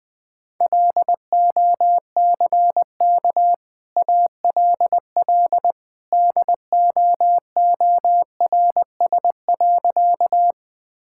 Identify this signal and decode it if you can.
Morse code, heard as a keyed tone, about 20 wpm, 700 Hz. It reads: LOCK ALL DOORS.
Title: Easy Sentences with Hints - 20wpm